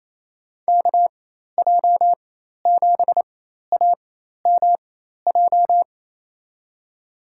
Morse code, heard as a keyed tone, about 28 wpm, spaced as for 13 wpm, 700 Hz. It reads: KJ7AMJ
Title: USA Call Signs - 6 Characters - 4x Character Spacing - 28wpm